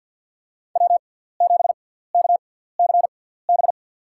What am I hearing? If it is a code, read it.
Morse code, heard as a keyed tone, about 53 wpm, spaced as for 17 wpm, 700 Hz. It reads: W7KCB
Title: USA Call Signs - 6x Character Spacing - 50wpm